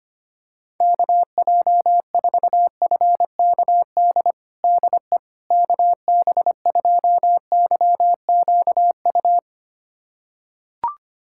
Morse code, heard as a keyed tone, about 25 wpm, 700 Hz. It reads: KJ4FKD DE KB2YQU
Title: USA Callsign DE Callsign 25wpm